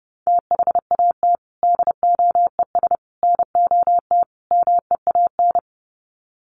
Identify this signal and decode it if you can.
Morse code, heard as a keyed tone, about 30 wpm, 700 Hz. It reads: THAT DOES NOT MEAN
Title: Sets of 4 Words 30wpm